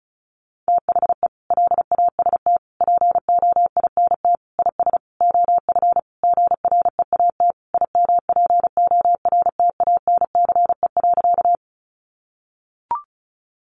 Morse code, heard as a keyed tone, about 34 wpm, 700 Hz. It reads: THE LAST POINT IS OF GREAT IMPORTANCE.
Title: Sentences from Top 1200 Words 35wpm